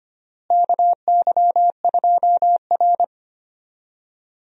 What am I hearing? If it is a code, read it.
Morse code, heard as a keyed tone, about 25 wpm, 700 Hz. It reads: KY2R
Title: USA Call Signs - 4 Characters - Repeated 3x - 25wpm